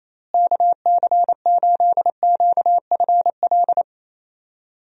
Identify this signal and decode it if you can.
Morse code, heard as a keyed tone, about 28 wpm, 700 Hz. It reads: KC8QFL
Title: USA Call Signs - 6 Characters - Repeated 3x - 28wpm